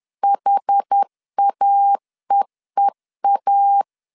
Morse code, heard as a keyed tone, about 10 wpm, 800 Hz. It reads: HAEEA
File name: CW / DET / 1 3 1500C S B 10wpm 3335